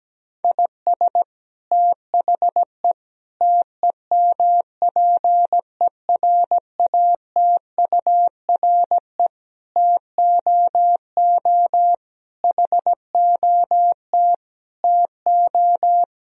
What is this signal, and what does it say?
Morse code, heard as a keyed tone, about 17 wpm, 700 Hz. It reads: IS THE TEMPERATURE TOO HOT TO
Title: Sentences from Top 1000 Words - Review Part 4 - 17wpm